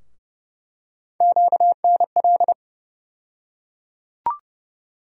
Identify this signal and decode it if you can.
Morse code, heard as a keyed tone, about 30 wpm, 700 Hz. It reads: QNL
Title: USA Call Sign Postfixes - 3 Letters 30wpm